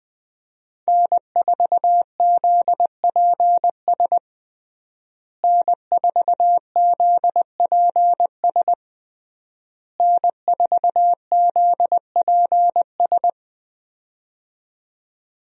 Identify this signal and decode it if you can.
Morse code, heard as a keyed tone, about 21 wpm, 700 Hz. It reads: N4ZPS N4ZPS N4ZPS
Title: USA Call Signs - 5 Characters - Repeated 3x - 20wpm